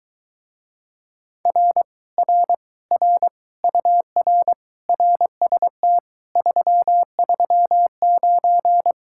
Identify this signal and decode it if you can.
Morse code, heard as a keyed tone, about 22 wpm, 700 Hz. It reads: R R R UR RST 339